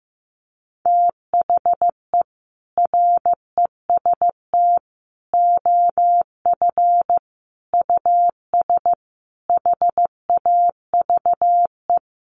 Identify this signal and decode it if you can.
Morse code, heard as a keyed tone, about 15 wpm, 700 Hz. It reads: THE REST OF US HAVE
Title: Sentences from Top 700 Words 15wpm